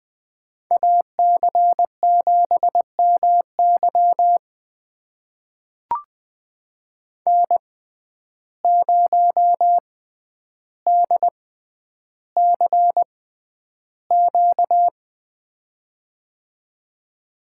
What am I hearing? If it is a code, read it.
Morse code, heard as a keyed tone, about 20 wpm, spaced as for 10 wpm, 700 Hz. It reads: AC7MY N0DCQ
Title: USA Call Signs - 5 Characters - 6x Character Spacing - 20wpm